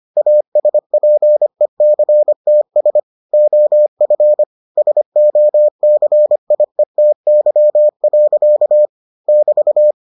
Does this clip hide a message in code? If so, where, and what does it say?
Yes: Morse code from 0.2 to 9.9 s, heard as a keyed tone, about 25 wpm, 600 Hz. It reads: ASPECTS OF SOCIETY. <BT>